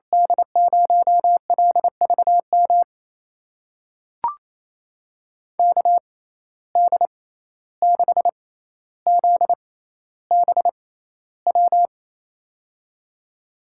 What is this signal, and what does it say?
Morse code, heard as a keyed tone, about 28 wpm, spaced as for 14 wpm, 700 Hz. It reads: D0LVM KD6ZBW
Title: USA Call Signs - 6 Characters - 6x Character Spacing - 28wpm